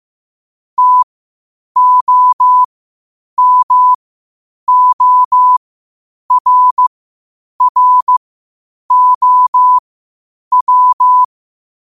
Morse code, heard as a keyed tone, about 15 wpm, spaced as for 8 wpm, 1000 Hz. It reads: TOMORROW
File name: qotdfiles / 8wpm